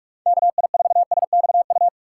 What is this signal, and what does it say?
Morse code, heard as a keyed tone, about 46 wpm, 700 Hz. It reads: KI4SXU
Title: USA Call Signs - 6 Characters - 2x Character Spacing - 45wpm